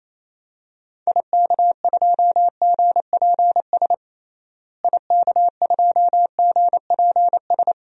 Morse code, heard as a keyed tone, about 28 wpm, 700 Hz. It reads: IK2GPS IK2GPS
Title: DXCC-100 Call Signs - 6 Characters - Repeated 2x - 28wpm